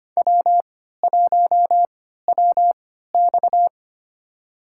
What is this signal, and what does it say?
Morse code, heard as a keyed tone, about 25 wpm, spaced as for 14 wpm, 700 Hz. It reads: W1WX
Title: USA Call Signs - 4 Characters - 3x Character Spacing - 25wpm